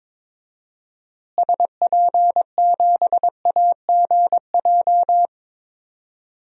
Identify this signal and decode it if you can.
Morse code, heard as a keyed tone, about 22 wpm, 700 Hz. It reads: SP7AGJ
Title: DXCC-100 Call Signs - 6 Characters - Repeated 2x - 22wpm